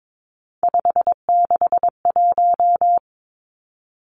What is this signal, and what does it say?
Morse code, heard as a keyed tone, about 22 wpm, 700 Hz. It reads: 561
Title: Numbers - 3-Digit 22wpm